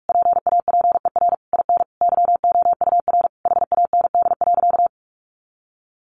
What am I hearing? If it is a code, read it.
Morse code, heard as a keyed tone, about 46 wpm, 700 Hz. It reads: PAPER IN YOUR HAND.
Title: Sentences from Top 400 Words 45wpm